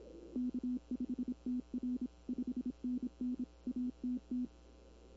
Morse code, heard as a keyed tone, about 26 wpm, 250 Hz. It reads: K5TR 5NN ATT
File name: CW / 2005-k5tr arrl cw dx -20:00